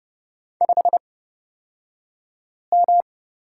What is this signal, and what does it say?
Morse code, heard as a keyed tone, about 29 wpm, spaced as for 5 wpm, 700 Hz. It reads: 5M
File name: Lesson 36d - BT - Two Characters 30wpm